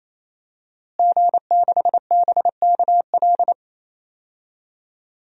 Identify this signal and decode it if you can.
Morse code, heard as a keyed tone, about 28 wpm, 700 Hz. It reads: G6BKL
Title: DXCC-100 Call Signs - 5 Characters - Repeated 3x - 28wpm